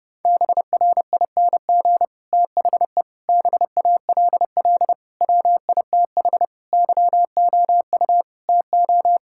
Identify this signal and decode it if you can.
Morse code, heard as a keyed tone, about 30 wpm, 700 Hz. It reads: BRING THE BALL WITH YOU TO
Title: Sentences from Top 1000 Words - Review Part 6 - 30wpm